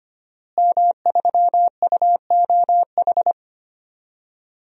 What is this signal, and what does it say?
Morse code, heard as a keyed tone, about 25 wpm, 700 Hz. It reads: M3UOH
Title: DXCC-100 Call Signs - 5 Characters - Repeated 2x - 25wpm